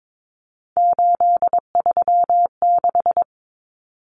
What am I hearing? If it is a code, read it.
Morse code, heard as a keyed tone, about 22 wpm, 700 Hz. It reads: 836
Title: Common Cut Numbers - 3-Digit 22wpm